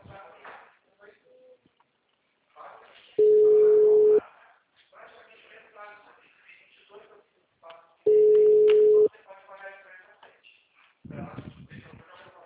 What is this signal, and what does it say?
Ringback tone (Europe, 425 Hz), with speech.
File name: out- -2002- - - .37148